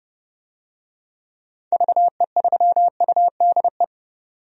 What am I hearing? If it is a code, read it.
Morse code, heard as a keyed tone, about 30 wpm, 700 Hz. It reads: VE3UDE
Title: DXCC-100 Call Signs - 6 Characters - 30wpm